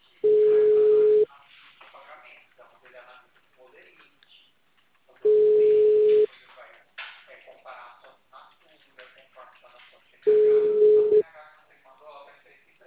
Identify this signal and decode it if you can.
Ringback tone (Europe, 425 Hz), with speech.